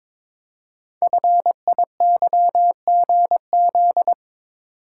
Morse code, heard as a keyed tone, about 22 wpm, 700 Hz. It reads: FIYGZ
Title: Code Groups - 5 Letters - 22wpm